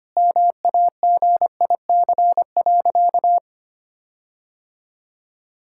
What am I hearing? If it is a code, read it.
Morse code, heard as a keyed tone, about 25 wpm, 700 Hz. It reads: MAGIC.
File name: practice / 25wpm / Sentences with 4-Word Phrases - 25wpm